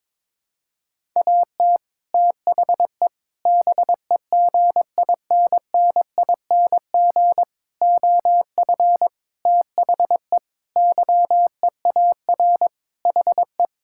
Morse code, heard as a keyed tone, about 22 wpm, 700 Hz. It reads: AT THE BEGINNING OF THE YEAR HE
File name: practice / 22wpm / Sentences with 4-Word Phrases - 22wpm